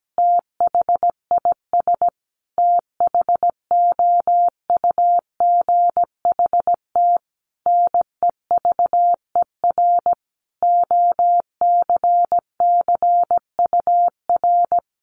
Morse code, heard as a keyed tone, about 17 wpm, 700 Hz. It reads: THIS THOUGHT NEVER OCCUR
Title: Sentences from Top 900 Words 17wpm